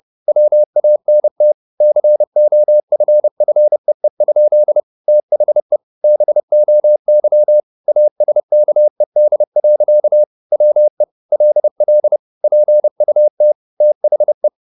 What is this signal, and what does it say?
Morse code, heard as a keyed tone, about 30 wpm, 600 Hz. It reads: WANT COFFEE? THE BOY ASKED. WE LL PUT THE
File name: the old man and the sea / the old man and the sea 1 30wpm